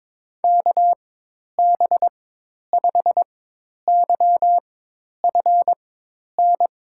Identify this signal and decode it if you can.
Morse code, heard as a keyed tone, about 22 wpm, spaced as for 10 wpm, 700 Hz. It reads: KB5YFN
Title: USA Call Signs - 6 Characters - 4x Character Spacing - 22wpm